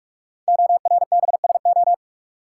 Morse code, heard as a keyed tone, about 44 wpm, 700 Hz. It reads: ORBSO
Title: Code Groups - 5 Letter:Numbers - 45wpm